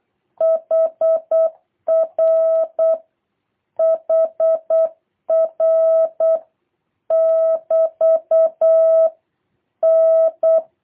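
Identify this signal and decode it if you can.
Morse code, heard as a keyed tone, about 8 wpm, 650 Hz. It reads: HR HR <BT> N
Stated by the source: Morse copy drill by JH8CBH: 24 random characters, A, E, N, T six times each, 3-second gaps for seven questions, 2-second gaps for three.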